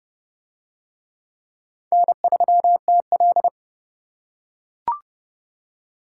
Morse code, heard as a keyed tone, about 28 wpm, 700 Hz. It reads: N3TL